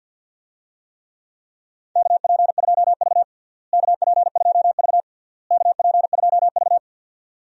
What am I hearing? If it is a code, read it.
Morse code, heard as a keyed tone, about 47 wpm, 700 Hz. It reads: KP2V KP2V KP2V
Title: DXCC-100 Call Signs - 4 Characters - Repeated 3x - 50wpm